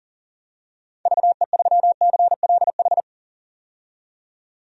Morse code, heard as a keyed tone, about 40 wpm, 700 Hz. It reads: VE3CLH
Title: DXCC-100 Call Signs - 6 Characters - 40wpm